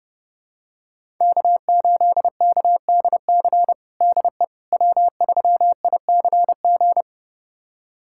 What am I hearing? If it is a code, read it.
Morse code, heard as a keyed tone, about 30 wpm, 700 Hz. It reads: K8KDC DE W3ICG
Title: USA Callsign DE Callsign - 5 Characters - 30wpm